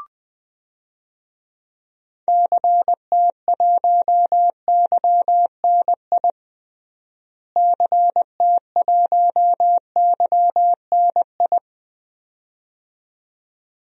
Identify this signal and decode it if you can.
Morse code, heard as a keyed tone, about 21 wpm, 700 Hz. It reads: CT1YNI CT1YNI
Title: DXCC-100 Call Signs - 6 Characters - Repeated 2x - 20wpm